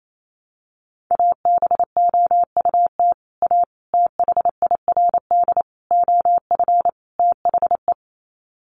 Morse code, heard as a keyed tone, about 28 wpm, 700 Hz. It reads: ABOUT A THIRD OF THE
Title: Sets of 5 Words Encore2 28wpm